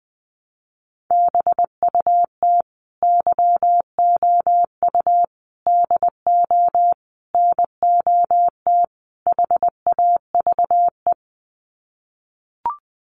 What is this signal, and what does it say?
Morse code, heard as a keyed tone, about 20 wpm, 700 Hz. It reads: BUT YOU DO NOT HAVE
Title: Sets of 5 Words Encore 20wpm